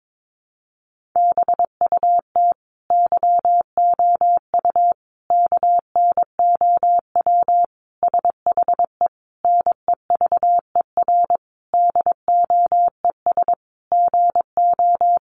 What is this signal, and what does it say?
Morse code, heard as a keyed tone, about 22 wpm, 700 Hz. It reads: BUT YOU KNOW SHE NEVER DOES GO
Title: Sentences from Top 1400 Words 22wpm